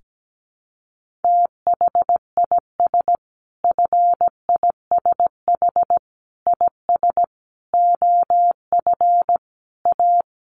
Morse code, heard as a keyed tone, about 17 wpm, 700 Hz. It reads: THIS FISH IS OF A
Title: Sentences from Top 1500 Words 17wpm